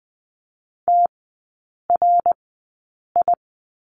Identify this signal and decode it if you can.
Morse code, heard as a keyed tone, about 20 wpm, spaced as for 8 wpm, 700 Hz. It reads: TRI
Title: 067 Lesson 12e - Letter d - Three Characters 20wpm